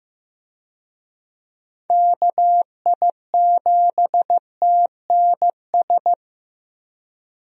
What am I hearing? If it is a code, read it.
Morse code, heard as a keyed tone, about 15 wpm, 700 Hz. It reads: KI7TNS
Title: USA Call Signs - 6 Characters - Repeated 3x - 15wpm